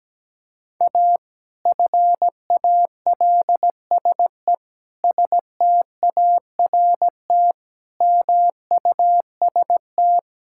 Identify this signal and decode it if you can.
Morse code, heard as a keyed tone, about 17 wpm, 700 Hz. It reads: A FALSE START MUST
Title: Sentences from Top 1600 Words 17wpm